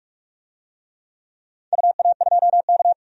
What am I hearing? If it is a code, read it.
Morse code, heard as a keyed tone, about 46 wpm, 700 Hz. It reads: UA2K
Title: DXCC-100 Call Signs - 4 Characters - 45wpm